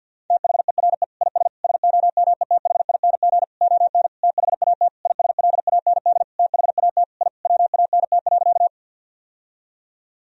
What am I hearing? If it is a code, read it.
Morse code, heard as a keyed tone, about 49 wpm, 700 Hz. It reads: THERE IS SOMETHING ON THAT ISLAND THAT I WANT.
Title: Sentences from Top 1000 Words - Review Part 3 - 50wpm